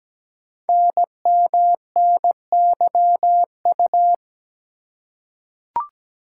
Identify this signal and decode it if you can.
Morse code, heard as a keyed tone, about 17 wpm, 700 Hz. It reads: NMNYU
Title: Code Groups - 5 Letters - 17wpm